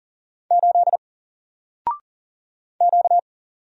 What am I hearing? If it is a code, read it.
Morse code, heard as a keyed tone, about 40 wpm, 700 Hz. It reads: 8 Q
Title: Single Letter-Number Alt 40wpm